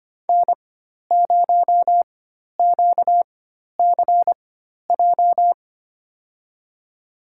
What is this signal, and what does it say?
Morse code, heard as a keyed tone, about 25 wpm, spaced as for 12 wpm, 700 Hz. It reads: N0QCJ
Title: USA Call Signs - 5 Characters - 4x Character Spacing - 25wpm